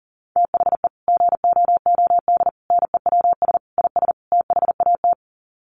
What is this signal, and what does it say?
Morse code, heard as a keyed tone, about 40 wpm, 700 Hz. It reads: THE GOOD NEWS IS THAT
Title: Sets of 5 Words 40wpm